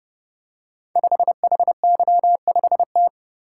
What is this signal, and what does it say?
Morse code, heard as a keyed tone, about 30 wpm, 700 Hz. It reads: 5HY5T
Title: Code Groups - 5 Letter:Numbers - 30wpm